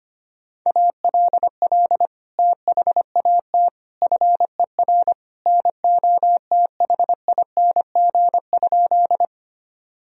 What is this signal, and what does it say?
Morse code, heard as a keyed tone, about 25 wpm, 700 Hz. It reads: ALL THAT FER NOTHING?
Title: Sentences from Top 500 Words and Abbreviations 25wpm